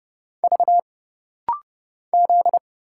Morse code, heard as a keyed tone, about 29 wpm, 700 Hz. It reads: V Z
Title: Single Letters - Rapid-Fire 30wpm